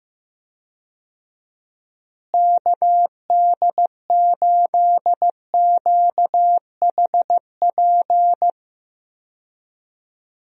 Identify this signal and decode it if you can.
Morse code, heard as a keyed tone, about 15 wpm, 700 Hz. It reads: KD8QHP
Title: USA Call Signs - 6 Characters - Repeated 2x - 15wpm